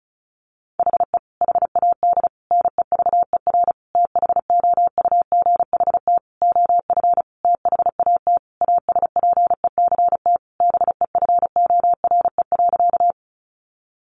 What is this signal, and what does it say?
Morse code, heard as a keyed tone, about 34 wpm, 700 Hz. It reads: HE HAD NEVER THOUGHT OF THAT ASPECT BEFORE.
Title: Sentences from Top 1300 Words 35wpm